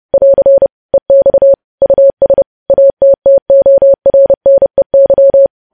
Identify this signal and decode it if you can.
Morse code, heard as a keyed tone, about 30 wpm, 550 Hz. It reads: <AR> EX US ATTORNEY
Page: news headlines in morse code at 30 wpm